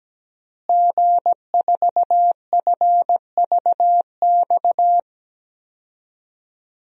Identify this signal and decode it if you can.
Morse code, heard as a keyed tone, about 17 wpm, 700 Hz. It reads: G4FVX
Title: DXCC-100 Call Signs - 5 Characters - Repeated 2x - 17wpm